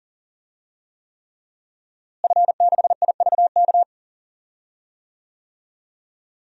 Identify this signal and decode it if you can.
Morse code, heard as a keyed tone, about 40 wpm, 700 Hz. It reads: F6IVK
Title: DXCC-100 Call Signs - 5 Characters - Repeated 3x - 40wpm